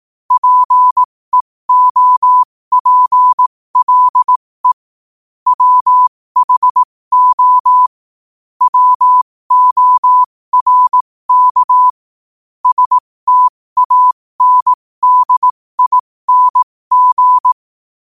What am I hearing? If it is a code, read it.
Morse code, heard as a keyed tone, about 18 wpm, spaced as for 15 wpm, 1000 Hz. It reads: PEOPLE WHO WORK STANDING
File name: qotdfiles / 15wpm